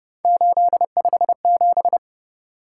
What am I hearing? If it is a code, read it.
Morse code, heard as a keyed tone, about 30 wpm, 700 Hz. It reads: 857
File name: Speed-Racing - Three Letter-Number Combinations - 25wpm